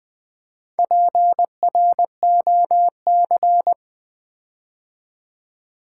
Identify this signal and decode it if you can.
Morse code, heard as a keyed tone, about 20 wpm, 700 Hz. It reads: PROC
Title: Build A Word - 4 Letters At A Time - 8-Letter Words - 20wpm